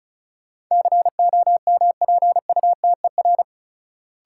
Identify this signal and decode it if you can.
Morse code, heard as a keyed tone, about 36 wpm, 700 Hz. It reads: COMPUTER
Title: Sentences from Top 1000 Words - Review Part 3 - New Word Drill - 35wpm